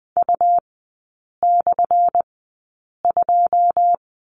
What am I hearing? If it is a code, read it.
Morse code, heard as a keyed tone, about 20 wpm, spaced as for 8 wpm, 700 Hz. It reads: U/2
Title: 178 Lesson 29e - Slash - Three Characters 20wpm